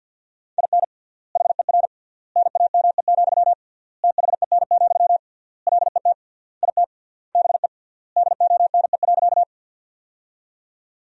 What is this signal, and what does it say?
Morse code, heard as a keyed tone, about 50 wpm, 700 Hz. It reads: IN HER NAME, THEN, LET IT BE DONE.
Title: 3x Word-Spacing - Sentences from Top 500 Words - 50wpm